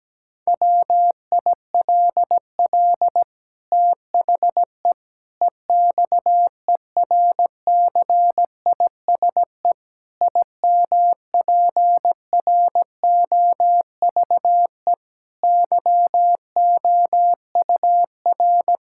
Morse code, heard as a keyed tone, about 17 wpm, 700 Hz. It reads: WILL THE EXERCISE IMPROVE YOUR
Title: Sentences from Top 1000 Words - Review Part 6 - 17wpm